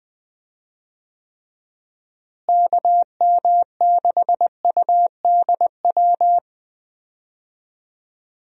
Morse code, heard as a keyed tone, about 21 wpm, 700 Hz. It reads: KM6UDW
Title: USA Call Signs - 6 Characters - Repeated 3x - 20wpm